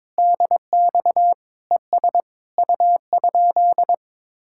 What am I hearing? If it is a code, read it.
Morse code, heard as a keyed tone, about 22 wpm, 700 Hz. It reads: DX ES U?
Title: Speed-Racing Common QSO Phrases Encore - 22wpm